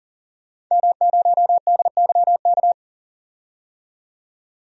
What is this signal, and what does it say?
Morse code, heard as a keyed tone, about 40 wpm, 700 Hz. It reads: M0DYK
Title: DXCC-100 Call Signs - 5 Characters - 40wpm